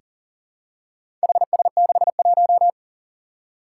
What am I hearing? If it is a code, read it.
Morse code, heard as a keyed tone, about 40 wpm, 700 Hz. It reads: HS61
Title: Four Challenging Letter-Number Combinations 40wpm